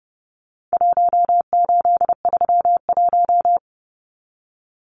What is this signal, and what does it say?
Morse code, heard as a keyed tone, about 30 wpm, 700 Hz. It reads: 1831
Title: Numbers - Contest Style - 4-Digit 30wpm